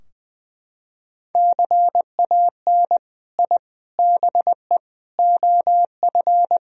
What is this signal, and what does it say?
Morse code, heard as a keyed tone, about 20 wpm, 700 Hz. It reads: CAN I BE OF
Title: Sentences from Top 1000 Words - Review Part 4 - 20wpm